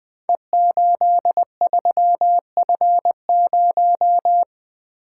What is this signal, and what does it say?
Morse code, heard as a keyed tone, about 20 wpm, 700 Hz. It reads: E83F0
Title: Code Groups - 5 Letter:Numbers - 20wpm